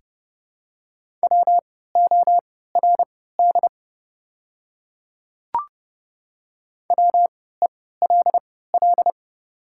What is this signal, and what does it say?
Morse code, heard as a keyed tone, about 31 wpm, spaced as for 17 wpm, 700 Hz. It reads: WORD WELL
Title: 3x Character Spacing - Top 500 Words - 30wpm